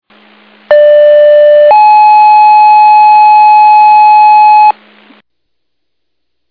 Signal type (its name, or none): two-tone page